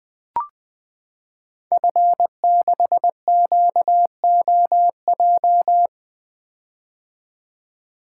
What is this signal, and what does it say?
Morse code, heard as a keyed tone, about 20 wpm, 700 Hz. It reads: F6QOJ
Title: Code Groups - 5 Letter:Numbers - 20wpm